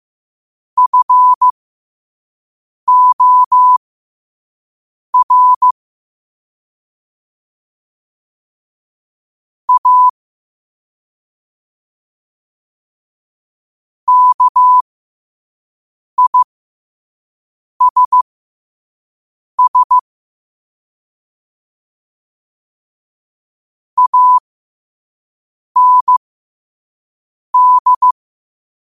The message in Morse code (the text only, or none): FOR A KISS AND